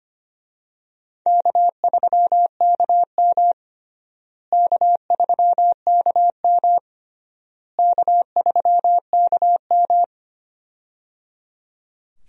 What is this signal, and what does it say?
Morse code, heard as a keyed tone, about 25 wpm, 700 Hz. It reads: K3KM K3KM K3KM
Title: USA Call Signs - 4 Characters - Repeated 3x - 25wpm